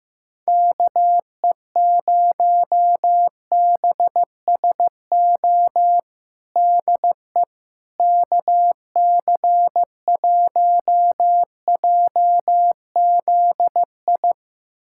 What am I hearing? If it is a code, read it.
Morse code, heard as a keyed tone, about 15 wpm, 700 Hz. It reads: KE0BSO DE KC1JZI
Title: USA Callsign DE Callsign - 6 Characters - 15wpm